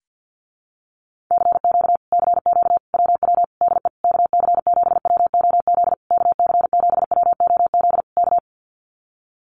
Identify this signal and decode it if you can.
Morse code, heard as a keyed tone, about 48 wpm, 700 Hz. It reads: CQ CQ WW DE KC7WOZ KC7WOZ K
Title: Common QSO Phrases Encore 50wpm